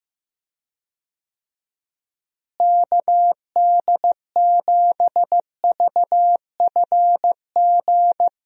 Morse code, heard as a keyed tone, about 15 wpm, 700 Hz. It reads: KD7VFG
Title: USA Call Signs - 6 Characters - Repeated 2x - 15wpm